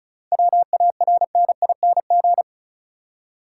Morse code, heard as a keyed tone, about 36 wpm, 700 Hz. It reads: WARNING
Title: Seven Letter Words - 35wpm